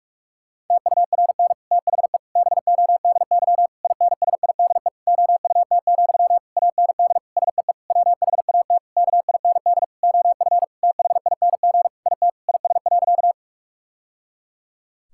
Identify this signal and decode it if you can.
Morse code, heard as a keyed tone, about 41 wpm, 700 Hz. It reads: TURN THE BODY INSIDE OUT, AND SEE WHAT KIND OF THING IT IS.
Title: Sentences from Top 1000 Words - Review Part 6 - 45wpm